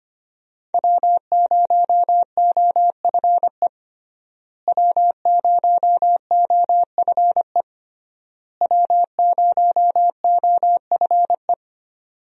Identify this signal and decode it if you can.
Morse code, heard as a keyed tone, about 25 wpm, 700 Hz. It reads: W0OFE W0OFE W0OFE
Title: USA Call Signs - 5 Characters - Repeated 3x - 25wpm